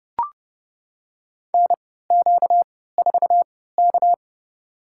Morse code, heard as a keyed tone, about 32 wpm, spaced as for 17 wpm, 700 Hz. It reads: NQ4K
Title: USA Call Signs - 4 Characters - 3x Character Spacing - 30wpm